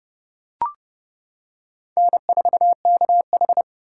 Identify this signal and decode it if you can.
Morse code, heard as a keyed tone, about 30 wpm, 700 Hz. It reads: N4KH
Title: K1USN SST QSO Practice - 30wpm